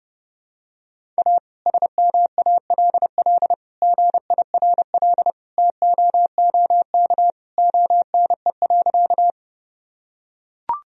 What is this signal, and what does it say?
Morse code, heard as a keyed tone, about 30 wpm, 700 Hz. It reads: A SMALL GIRL TOOK ONE.
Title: Sentences from Top 1000 Words - Review Part 1 - 30wpm